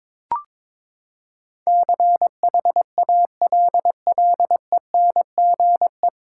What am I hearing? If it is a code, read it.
Morse code, heard as a keyed tone, about 22 wpm, 700 Hz. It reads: CHALLENGE
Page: Morse Code Ninja practice 22wpm